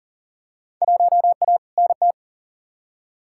Morse code, heard as a keyed tone, about 40 wpm, 700 Hz. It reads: 1A NT
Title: Field Day Exchanges Encore - 40wpm